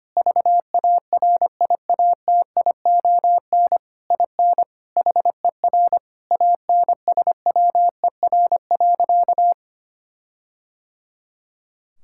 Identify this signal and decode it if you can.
Morse code, heard as a keyed tone, about 25 wpm, 700 Hz. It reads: VARIATION IN HER ANSWER.
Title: Sentences from Top 1600 Words 25wpm